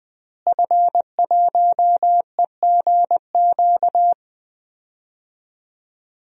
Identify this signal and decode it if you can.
Morse code, heard as a keyed tone, about 21 wpm, 700 Hz. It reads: F1EGQ
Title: DXCC-100 Call Signs - 5 Characters - Repeated 2x - 20wpm